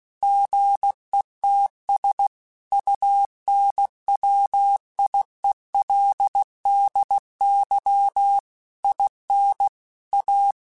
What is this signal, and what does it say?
Morse code, heard as a keyed tone, about 16 wpm, 800 Hz. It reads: GETS UNWIELDY IN A
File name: Technical Subjects / Technical Subjects-mp3s / AND Gate @15